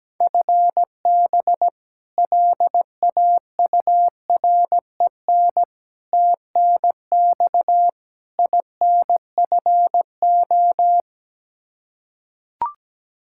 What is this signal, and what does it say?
Morse code, heard as a keyed tone, about 17 wpm, 700 Hz. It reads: FB LAUREN TNX INFO
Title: K1USN SST QSO Practice 17wpm